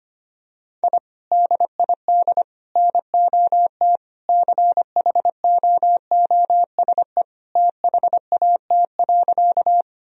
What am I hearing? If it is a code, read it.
Morse code, heard as a keyed tone, about 25 wpm, 700 Hz. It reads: I DID NOT CHOOSE THAT.
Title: Sentences from Top 1000 Words - Review Part 4 - 25wpm